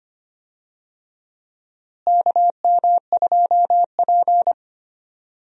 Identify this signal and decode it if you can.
Morse code, heard as a keyed tone, about 25 wpm, 700 Hz. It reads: KM2P
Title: USA Call Signs - 4 Characters - Repeated 3x - 25wpm